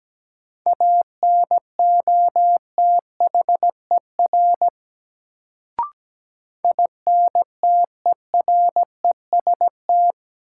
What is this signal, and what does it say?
Morse code, heard as a keyed tone, about 17 wpm, 700 Hz. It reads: ANOTHER INTEREST
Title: Sets of 5 Words Encore - New Word Drill - 17wpm